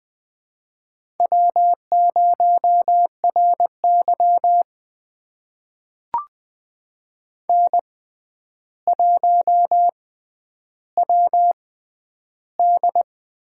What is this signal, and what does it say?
Morse code, heard as a keyed tone, about 20 wpm, spaced as for 10 wpm, 700 Hz. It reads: W0RY N1WD
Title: USA Call Signs - 4 Characters - 6x Character Spacing - 20wpm